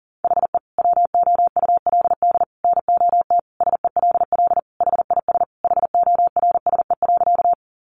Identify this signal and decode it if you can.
Morse code, heard as a keyed tone, about 40 wpm, 700 Hz. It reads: HE WOULD NOT SELL HIS HORSE.
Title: Sentences from Top 1100 Words 40wpm